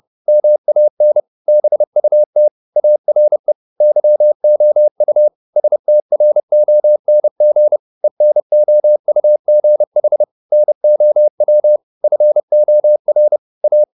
Morse code, heard as a keyed tone, about 30 wpm, 600 Hz. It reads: MAN BUT ARE YOU STRONG ENOUGH NOW FOR A
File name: the old man and the sea / the old man and the sea 1 30wpm